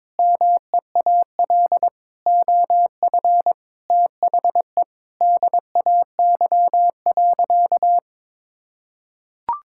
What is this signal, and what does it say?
Morse code, heard as a keyed tone, about 22 wpm, 700 Hz. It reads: MEAL OF THE DAY.
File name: Sentences with 9-Word Phrases - 22wpm